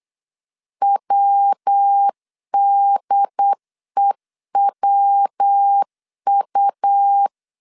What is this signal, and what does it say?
Morse code, heard as a keyed tone, about 8 wpm, 800 Hz. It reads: WDEWU